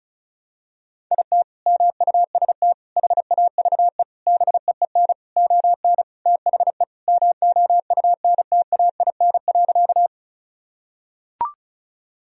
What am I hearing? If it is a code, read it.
Morse code, heard as a keyed tone, about 34 wpm, 700 Hz. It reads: IT MUST HAVE BEEN ON THE MOUNTAIN.
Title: Sentences from Top 1000 Words - Review Part 1 - 35wpm